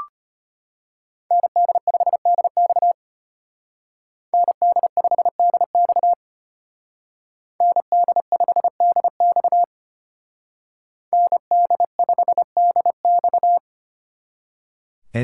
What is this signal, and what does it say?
Morse code, heard as a keyed tone, about 30 wpm, 700 Hz. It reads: ND5DX ND5DX ND5DX ND5DX
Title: Speed Racing - USA 2x2 Call Signs 25wpm